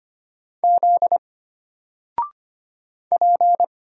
Morse code, heard as a keyed tone, about 25 wpm, 700 Hz. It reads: Z P
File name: Single-Letter-Number-25wpm